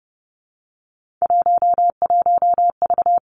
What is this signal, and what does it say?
Morse code, heard as a keyed tone, about 30 wpm, 700 Hz. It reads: 11V
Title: Three Challenging Letter-Number Combinations 30wpm